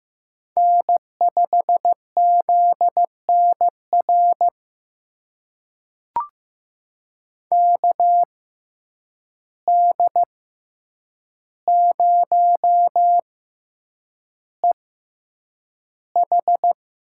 Morse code, heard as a keyed tone, about 15 wpm, spaced as for 8 wpm, 700 Hz. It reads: N5ZNR KD0EH
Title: USA Call Signs - 5 Characters - 6x Character Spacing - 15wpm